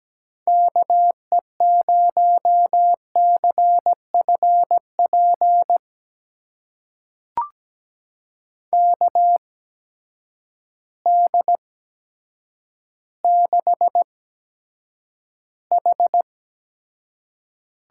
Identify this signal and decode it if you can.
Morse code, heard as a keyed tone, about 17 wpm, spaced as for 8 wpm, 700 Hz. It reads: KE0CFP KD6H
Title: USA Call Signs - 8x Character Spacing - 17wpm